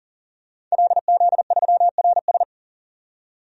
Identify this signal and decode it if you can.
Morse code, heard as a keyed tone, about 40 wpm, 700 Hz. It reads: LZ3RS
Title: DXCC-100 Call Signs - 5 Characters - 40wpm